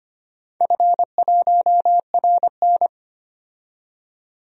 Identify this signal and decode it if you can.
Morse code, heard as a keyed tone, about 25 wpm, 700 Hz. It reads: F1RN
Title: DXCC-100 Call Signs - 4 Characters - 25wpm